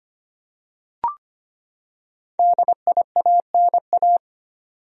Morse code, heard as a keyed tone, about 25 wpm, 700 Hz. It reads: DIANA
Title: USA Common Names - Max 5 Letters - 25wpm